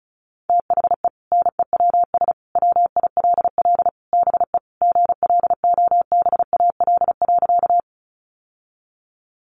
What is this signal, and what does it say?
Morse code, heard as a keyed tone, about 34 wpm, 700 Hz. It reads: THE NEWS WILL BE GLOBAL.
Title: Sentences from Top 1200 Words 35wpm